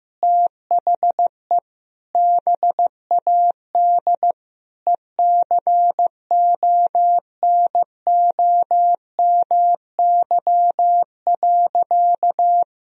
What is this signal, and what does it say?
Morse code, heard as a keyed tone, about 15 wpm, 700 Hz. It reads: THE BAD ECONOMY.
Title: Sentences from Top 1000 Words - Review Part 3 - 15wpm